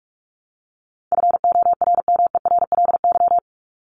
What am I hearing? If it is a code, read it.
Morse code, heard as a keyed tone, about 44 wpm, 700 Hz. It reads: FORMERLY
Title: LY Suffix - Top 100 Words with Base Words 45wpm